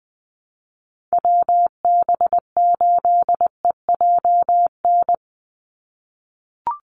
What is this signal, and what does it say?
Morse code, heard as a keyed tone, about 20 wpm, 700 Hz. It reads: WB8EJN
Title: Call Signs - 20wpm